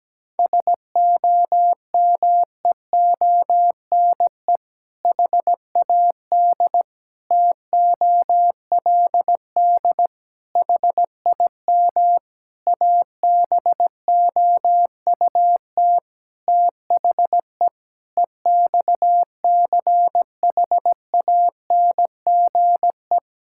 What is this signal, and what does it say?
Morse code, heard as a keyed tone, about 17 wpm, 700 Hz. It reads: SOMEONE HAD TOLD HIM ABOUT THE EXCHANGE
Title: Sentences from Top 1000 Words - Review Part 6 - 17wpm